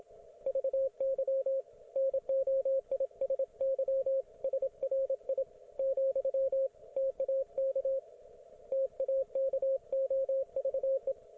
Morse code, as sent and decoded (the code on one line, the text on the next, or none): ...- -.-- / -. --- .. ... -.-- / ... .-. .. / --..-- / - .- -.- / - .- -.- --- ...- .
VY NOISY SRI , TAK TAKOVE